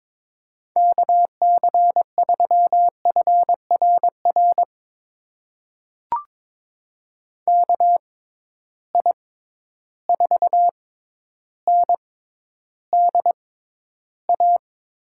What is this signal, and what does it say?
Morse code, heard as a keyed tone, about 22 wpm, spaced as for 11 wpm, 700 Hz. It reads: KC3FRR KI4NDA
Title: USA Call Signs - 6 Characters - 6x Character Spacing - 22wpm